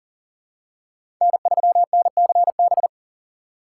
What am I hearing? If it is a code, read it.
Morse code, heard as a keyed tone, about 40 wpm, 700 Hz. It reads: N3NCB
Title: USA Call Signs - 5 Characters - 3x Character Spacing - 40wpm